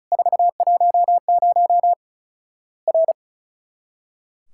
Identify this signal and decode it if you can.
Morse code, heard as a keyed tone, about 36 wpm, 700 Hz. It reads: 410 R
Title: Worked All Europe DX Contest - QTC Exchange - 35wpm